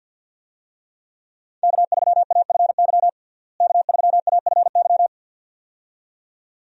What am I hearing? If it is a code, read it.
Morse code, heard as a keyed tone, about 50 wpm, 700 Hz. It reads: K3AFY K3AFY
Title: USA Call Signs - 5 Characters - Repeated 2x - 50wpm